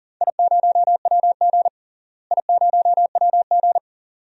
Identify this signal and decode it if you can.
Morse code, heard as a keyed tone, about 37 wpm, 700 Hz. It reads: I0WG I0WG
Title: DXCC-100 Call Signs - 4 Characters - Repeated 2x - 40wpm